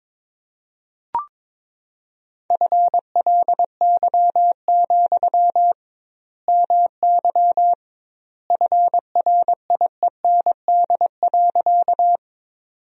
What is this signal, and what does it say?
Morse code, heard as a keyed tone, about 22 wpm, 700 Hz. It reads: FLY, MY FRIEND.
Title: 2x Word-Spacing - Sentences from Top 500 Words - 22wpm